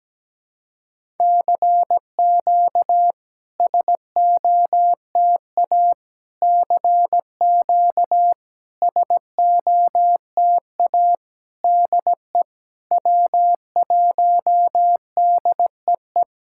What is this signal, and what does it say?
Morse code, heard as a keyed tone, about 17 wpm, 700 Hz. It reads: CQ SOTA CQ SOTA DE W1DEE
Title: SOTA QSO Practice 17wpm